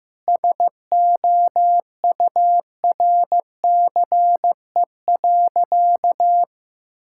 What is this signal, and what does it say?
Morse code, heard as a keyed tone, about 15 wpm, 700 Hz. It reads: SOURCE.
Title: Sentences from Top 1000 Words - Review Part 5 - 15wpm